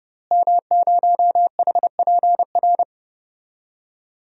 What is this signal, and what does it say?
Morse code, heard as a keyed tone, about 30 wpm, 700 Hz. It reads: M0HPR
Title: DXCC-100 Call Signs - 5 Characters - Repeated 2x - 30wpm